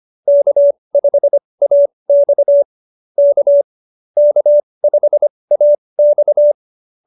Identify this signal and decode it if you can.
Morse code, heard as a keyed tone, about 25 wpm, spaced as for 20 wpm, 590 Hz. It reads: K5AX K K5AX